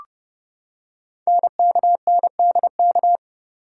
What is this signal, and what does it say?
Morse code, heard as a keyed tone, about 29 wpm, 700 Hz. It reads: NKNDK
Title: Code Groups - 5 Letters - 30wpm